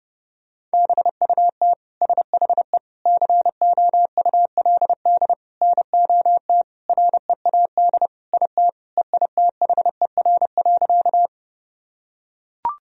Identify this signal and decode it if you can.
Morse code, heard as a keyed tone, about 30 wpm, 700 Hz. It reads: BUT SHE COULD NOT READ IT EITHER.
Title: Sentences from Top 1000 Words - Review Part 4 - 30wpm